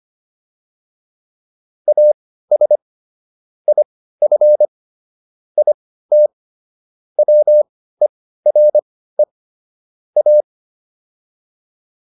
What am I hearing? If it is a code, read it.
Morse code, heard as a keyed tone, about 25 wpm, spaced as for 15 wpm, 600 Hz. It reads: AS IF IT WERE A
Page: intermediate-practice-files